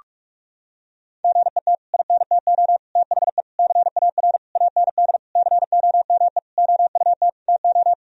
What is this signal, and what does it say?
Morse code, heard as a keyed tone, about 46 wpm, 700 Hz. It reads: GET INTO THE CAR AND COME OUT TO
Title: Sentences from Top 1000 Words - Review Part 1 - 45wpm